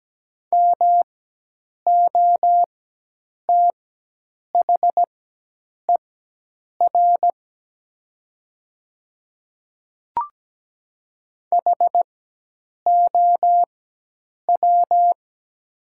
Morse code, heard as a keyed tone, about 17 wpm, spaced as for 8 wpm, 700 Hz. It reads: MOTHER HOW
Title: 4x Character Spacing - Top 300 Words - 17wpm